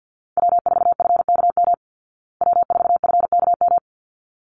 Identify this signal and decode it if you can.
Morse code, heard as a keyed tone, about 50 wpm, 700 Hz. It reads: W4FKM W4FKM
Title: Call Signs Repeated 3x - 50wpm